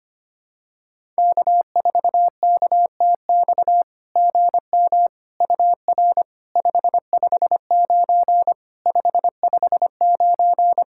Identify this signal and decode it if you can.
Morse code, heard as a keyed tone, about 25 wpm, 690 Hz. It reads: K4KTX GM UR 559 559
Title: SOTA QSO Practice 25wpm